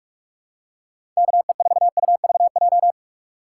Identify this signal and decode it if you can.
Morse code, heard as a keyed tone, about 42 wpm, 700 Hz. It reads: KE4UVJ
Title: USA Call Signs - 4x Character Spacing - 45wpm